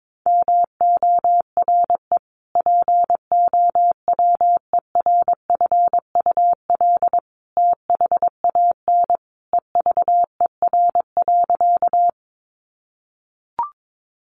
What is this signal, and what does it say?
Morse code, heard as a keyed tone, about 22 wpm, 700 Hz. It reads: MORE POWERFUL THAN EVER.
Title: Sentences from Top 1300 Words 22wpm